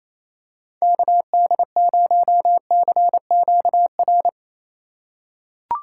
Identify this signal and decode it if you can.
Morse code, heard as a keyed tone, about 28 wpm, 700 Hz. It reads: KD0CQR
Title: USA Call Signs - 6 Characters - 3x Character Spacing - 28wpm